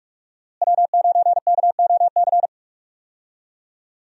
Morse code, heard as a keyed tone, about 44 wpm, 700 Hz. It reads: W9KOC
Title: USA Call Signs - 5 Characters - Repeated 3x - 45wpm